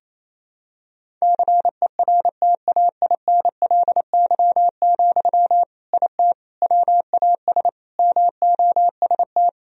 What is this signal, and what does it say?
Morse code, heard as a keyed tone, about 28 wpm, 700 Hz. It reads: CERTAINLY, IT WAS MOST
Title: Sentences from Top 1000 Words - Review Part 1 - 28wpm